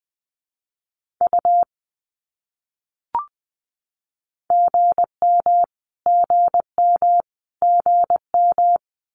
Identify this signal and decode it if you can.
Morse code, heard as a keyed tone, about 20 wpm, 700 Hz. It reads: U GM GM GM
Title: Abbreviations Repeated 3x - 20wpm